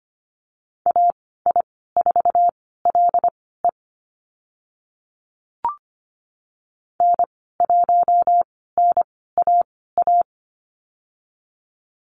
Morse code, heard as a keyed tone, about 25 wpm, spaced as for 16 wpm, 690 Hz. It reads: AI4LE N1NAA
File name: Lesson 15g - Number 5 - Call Signs 25wpm